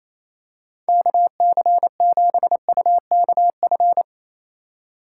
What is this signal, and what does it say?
Morse code, heard as a keyed tone, about 28 wpm, 700 Hz. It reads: KC7UKF
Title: USA Call Signs - 6 Characters - 3x Character Spacing - 28wpm